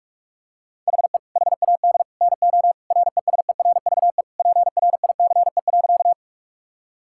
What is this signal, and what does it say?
Morse code, heard as a keyed tone, about 41 wpm, 700 Hz. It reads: HE HAD NO RESERVE PRICE.
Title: Sentences from Top 1000 Words - Review Part 6 - 45wpm